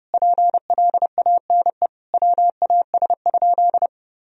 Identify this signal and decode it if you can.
Morse code, heard as a keyed tone, about 32 wpm, 700 Hz. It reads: PLANE WAS?
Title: Sentences from Top 1000 Words - Review Part 3 - 30wpm